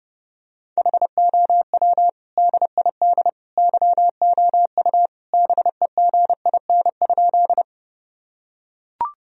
Morse code, heard as a keyed tone, about 30 wpm, 700 Hz. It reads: HOW DID YOU BEGIN?
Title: Sentences from Top 1000 Words - Review Part 1 - 30wpm